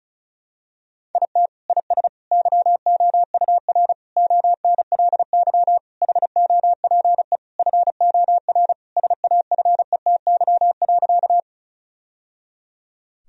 Morse code, heard as a keyed tone, about 34 wpm, 700 Hz. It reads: IT IS YOUR ONLY HOPE FOR SAFETY.
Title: Sentences from Top 1000 Words - Review Part 4 - 35wpm